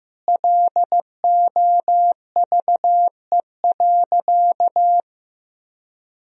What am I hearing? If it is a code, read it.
Morse code, heard as a keyed tone, about 15 wpm, 700 Hz. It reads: LOVE.
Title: Sentences from Top 1000 Words - Review Part 2 - 15wpm